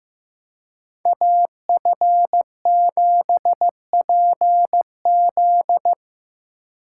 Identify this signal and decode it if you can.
Morse code, heard as a keyed tone, about 18 wpm, 700 Hz. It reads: AF7PZ
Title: Speed Racing - USA 2x2 Call Signs 15wpm